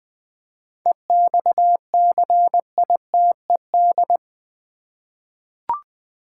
Morse code, heard as a keyed tone, about 20 wpm, 700 Hz. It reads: EXCITED
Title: 7 Letter Words - First 3 Letters Given - 20wpm